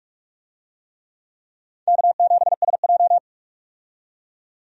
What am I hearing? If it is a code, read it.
Morse code, heard as a keyed tone, about 46 wpm, 700 Hz. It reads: K7SJ
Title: USA Call Signs - 4 Characters - 45wpm